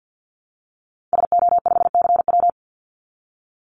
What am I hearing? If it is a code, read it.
Morse code, heard as a keyed tone, about 50 wpm, 700 Hz. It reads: SO5CW
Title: Call Signs Repeated 2x - 50wpm